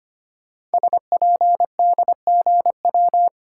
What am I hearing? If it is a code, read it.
Morse code, heard as a keyed tone, about 25 wpm, 700 Hz. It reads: SPDGW
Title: Code Groups - 5 Letters - 25wpm